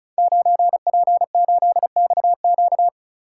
Morse code, heard as a keyed tone, about 36 wpm, 690 Hz. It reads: 9P8XQ
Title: Code Groups - 5 Letter:Numbers - 35wpm